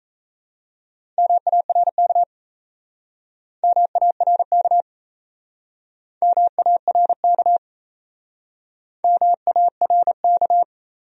Morse code, heard as a keyed tone, about 38 wpm, 700 Hz. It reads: MARK MARK MARK MARK